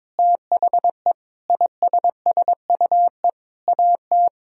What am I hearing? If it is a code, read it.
Morse code, heard as a keyed tone, about 22 wpm, 700 Hz. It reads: THE ISSUE AT
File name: Sentences with 9-Word Phrases - 22wpm